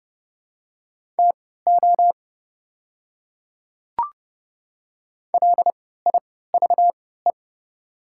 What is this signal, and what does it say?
Morse code, heard as a keyed tone, about 31 wpm, spaced as for 17 wpm, 700 Hz. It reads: TO LIVE
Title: 3x Character Spacing - Top 200 Words - 30wpm